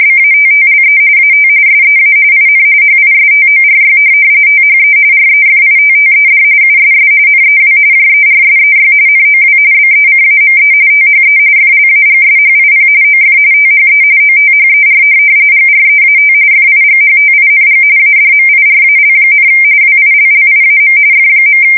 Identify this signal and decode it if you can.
RTTY, about 45.45 baud. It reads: BATTLE AT SEA IN HISTORY THE REFITTED USS NEW JERSEY NOW BECAME THE FLAGSHIP OF THE OTHER GREAT FIGHTING ADMIRAL, WILLIAM F.
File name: fsk170 / USS-New-Jersey